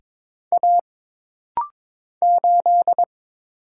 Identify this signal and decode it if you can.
Morse code, heard as a keyed tone, about 22 wpm, 700 Hz. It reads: A 8
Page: This rapid-fire timing is geared to push your ICR (Instant Character Recognition) skills.